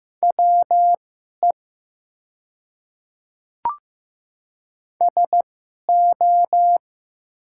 Morse code, heard as a keyed tone, about 15 wpm, spaced as for 11 wpm, 700 Hz. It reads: WE SO
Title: 2x Character Spacing - Top 100 Words - 15wpm